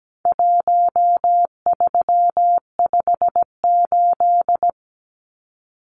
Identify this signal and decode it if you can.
Morse code, heard as a keyed tone, about 17 wpm, 700 Hz. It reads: 1358
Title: Numbers - Contest Style - 4-Digit 17wpm